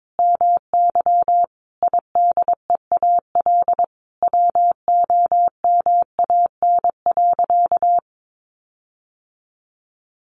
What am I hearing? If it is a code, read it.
Morse code, heard as a keyed tone, about 22 wpm, 700 Hz. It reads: MY IDEAL WOMAN.
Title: Sentences from Top 1400 Words 22wpm